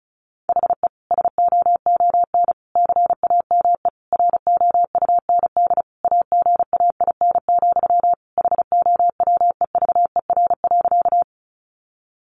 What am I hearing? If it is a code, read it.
Morse code, heard as a keyed tone, about 34 wpm, 700 Hz. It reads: HE SOON CAME ROUND AGAIN, HOWEVER.
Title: Sentences from Top 600 Words 35wpm